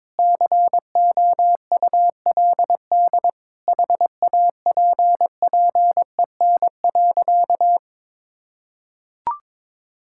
Morse code, heard as a keyed tone, about 22 wpm, 700 Hz. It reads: COULD HAPPEN.
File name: Sentences with 9-Word Phrases - 22wpm